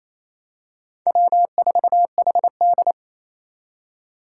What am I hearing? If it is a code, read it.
Morse code, heard as a keyed tone, about 28 wpm, 700 Hz. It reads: W4HD
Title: Four Challenging Letter-Number Combinations 28wpm